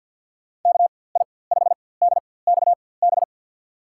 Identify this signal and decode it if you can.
Morse code, heard as a keyed tone, about 47 wpm, spaced as for 23 wpm, 700 Hz. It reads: KI5DXB
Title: USA Call Signs - 6 Characters - 4x Character Spacing - 50wpm